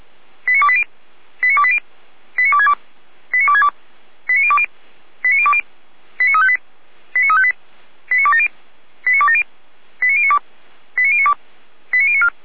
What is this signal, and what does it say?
Five-tone selective call (ZVEI), heard as quick runs of five tones, tones of about 70 ms. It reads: 89280 89280 89262 89262 89020 89020 89358 89358 89270 89270 89092 89092 89094
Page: SDR/ZVEI